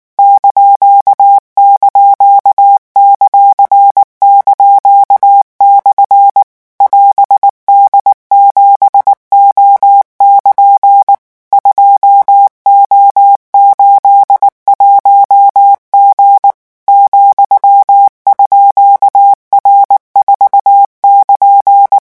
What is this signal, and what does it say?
Morse code, heard as a keyed tone, about 19 wpm, 800 Hz. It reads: ));)/ <AS>D7O<KN> 2O81G ,_R4<KN>
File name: morse mp3 / revision15-vitesse 18 mots